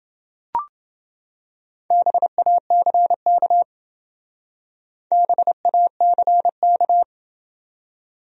Morse code, heard as a keyed tone, about 28 wpm, 700 Hz. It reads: BACK BACK